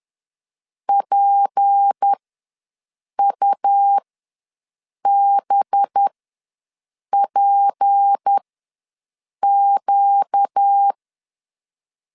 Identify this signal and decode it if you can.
Morse code, heard as a keyed tone, about 10 wpm, spaced as for 6 wpm, 800 Hz. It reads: PUBPQ
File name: LEZIONI / 6 PKX0 / 6 P ws 10wpm 3751